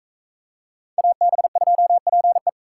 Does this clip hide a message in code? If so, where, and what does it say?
Yes: Morse code from 1.0 to 2.5 s, heard as a keyed tone, about 39 wpm, 700 Hz. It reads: AB2PE